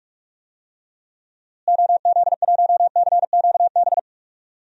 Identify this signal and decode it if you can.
Morse code, heard as a keyed tone, about 41 wpm, 700 Hz. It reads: OZ1CQB
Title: DXCC-100 Call Signs - 6 Characters - Repeated 3x - 45wpm